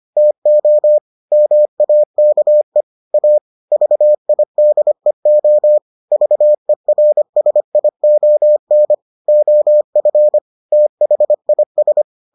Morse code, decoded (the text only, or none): TO MAKE A VIDEO VERSION OF THIS